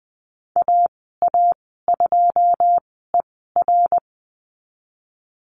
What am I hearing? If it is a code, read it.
Morse code, heard as a keyed tone, about 20 wpm, spaced as for 14 wpm, 700 Hz. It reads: AA2ER
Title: 140 Lesson 23g - Letter f - Call Signs 20wpm